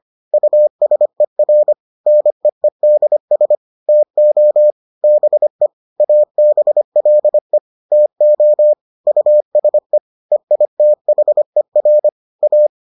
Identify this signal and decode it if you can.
Morse code, heard as a keyed tone, about 25 wpm, 600 Hz. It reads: USER NEEDS TO BE ABLE TO USE EITHER A